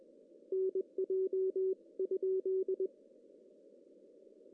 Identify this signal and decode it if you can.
Morse code, heard as a keyed tone, about 20 wpm, 380 Hz. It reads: NJ ?
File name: CW / cqww-cw-k5tr